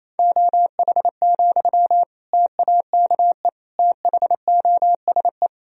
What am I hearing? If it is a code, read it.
Morse code, heard as a keyed tone, about 28 wpm, 700 Hz. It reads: OH, TAKE THOSE
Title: Sentences from Top 1000 Words - Review Part 1 - 28wpm